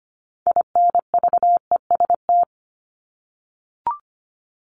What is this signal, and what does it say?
Morse code, heard as a keyed tone, about 25 wpm, 700 Hz. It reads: INVEST
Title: Top 1001-1250 Words 25wpm